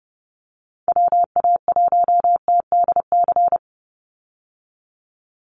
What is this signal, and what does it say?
Morse code, heard as a keyed tone, about 30 wpm, 700 Hz. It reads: WA1TDC
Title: USA Call Signs 30wpm